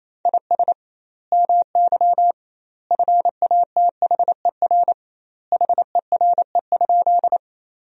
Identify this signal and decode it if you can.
Morse code, heard as a keyed tone, about 28 wpm, 700 Hz. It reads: IS MY FATHER HERE?
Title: 2x Word-Spacing - Sentences from Top 300 Words - 28wpm